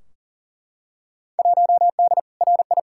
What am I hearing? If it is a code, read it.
Morse code, heard as a keyed tone, about 40 wpm, 700 Hz. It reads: 1D RI
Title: Field Day Exchanges - 40wpm